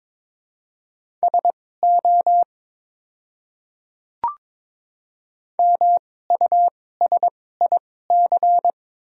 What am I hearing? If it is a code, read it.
Morse code, heard as a keyed tone, about 22 wpm, spaced as for 16 wpm, 700 Hz. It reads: SO MUSIC
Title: 2x Character Spacing - Top 300 Words - 22wpm